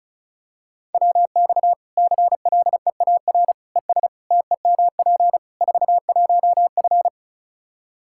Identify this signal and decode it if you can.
Morse code, heard as a keyed tone, about 36 wpm, 700 Hz. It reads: WX CLEAR ES TEMP 41F
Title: K1USN SST QSO Practice 35wpm